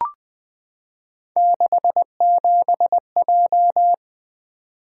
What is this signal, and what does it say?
Morse code, heard as a keyed tone, about 20 wpm, 700 Hz. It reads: 67J
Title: Three Letter-Number Combinations - 20wpm